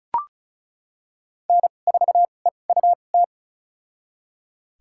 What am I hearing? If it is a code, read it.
Morse code, heard as a keyed tone, about 34 wpm, spaced as for 25 wpm, 700 Hz. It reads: N4EUT
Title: USA Call Signs - 5 Characters - 2x Character Spacing - 35wpm